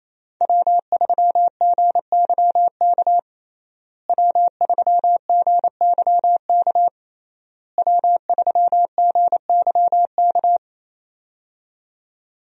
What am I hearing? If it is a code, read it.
Morse code, heard as a keyed tone, about 28 wpm, 700 Hz. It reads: W3GYK W3GYK W3GYK
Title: USA Call Signs - 5 Characters - Repeated 3x - 28wpm